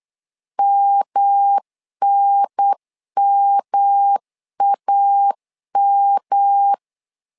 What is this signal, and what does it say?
Morse code, heard as a keyed tone, about 8 wpm, 800 Hz. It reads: MNMAM
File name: LEZIONI / 3 MWRO / 3 M 8wpm 2300